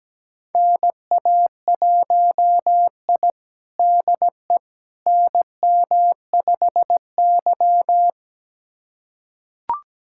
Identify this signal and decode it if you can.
Morse code, heard as a keyed tone, about 17 wpm, 700 Hz. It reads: NA1I DE NM5Y
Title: USA Callsign DE Callsign - 4 Characters - 17wpm